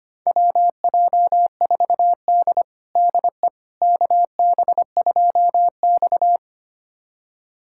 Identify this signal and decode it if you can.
Morse code, heard as a keyed tone, about 25 wpm, 700 Hz. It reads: WJ4D DE KB2X
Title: USA Callsign DE Callsign - 4 Characters - 25wpm